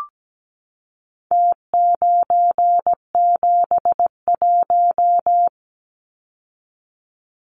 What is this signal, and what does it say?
Morse code, heard as a keyed tone, about 17 wpm, 700 Hz. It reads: T971
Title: Numbers - Contest Style - 4-Digit 17wpm